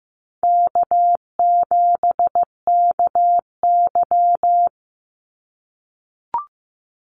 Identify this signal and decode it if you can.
Morse code, heard as a keyed tone, about 15 wpm, 700 Hz. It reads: K7KY
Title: Easy Call Signs - 15wpm